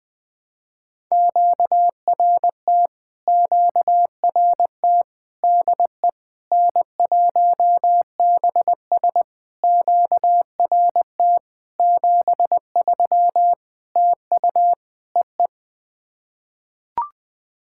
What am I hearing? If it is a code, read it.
Morse code, heard as a keyed tone, about 20 wpm, 660 Hz. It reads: QRT QRT DE N1BS QRT 73 TU EE
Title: POTA QSO Practice - 20wpm